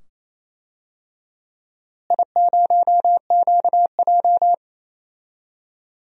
Morse code, heard as a keyed tone, about 28 wpm, 700 Hz. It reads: I0QJ
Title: DXCC-100 Call Signs - 4 Characters - Repeated 2x - 28wpm